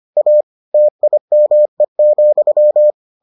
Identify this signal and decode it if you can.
Morse code, heard as a keyed tone, about 25 wpm, 600 Hz. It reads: A TIME,